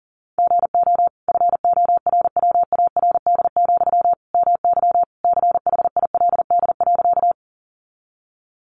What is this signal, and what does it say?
Morse code, heard as a keyed tone, about 40 wpm, 700 Hz. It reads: GO FORWARD, MY CHILD.
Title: Sentences from Top 1300 Words 40wpm